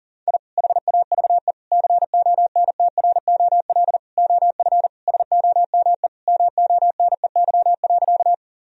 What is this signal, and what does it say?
Morse code, heard as a keyed tone, about 40 wpm, 700 Hz. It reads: I HAVE CONTROL OF SOME MONEY.
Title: Sentences from Top 1000 Words - Review Part 3 - 40wpm